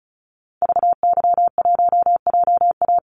Morse code, heard as a keyed tone, about 34 wpm, 700 Hz. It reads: VY1JA
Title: Call Signs Repeated 3x - 35wpm